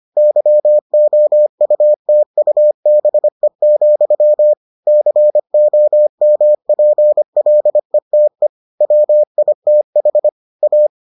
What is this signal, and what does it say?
Morse code, heard as a keyed tone, about 25 wpm, 600 Hz. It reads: YOUTUBE, COMPLETE WITH A